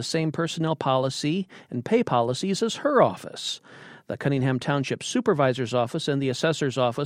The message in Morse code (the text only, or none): none